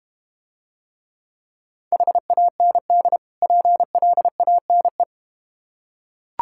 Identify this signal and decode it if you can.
Morse code, heard as a keyed tone, about 31 wpm, 700 Hz. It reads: HAND PLANE